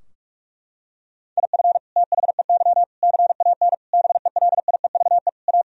Morse code, heard as a keyed tone, about 46 wpm, 700 Hz. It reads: IF THEY CAN BELIEVE A
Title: Sentences from Top 1000 Words - Review Part 2 - 45wpm